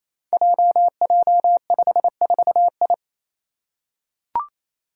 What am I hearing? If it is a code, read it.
Morse code, heard as a keyed tone, about 28 wpm, 700 Hz. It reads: JJ54I
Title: Five Challenging Letter-Number Combinations 28wpm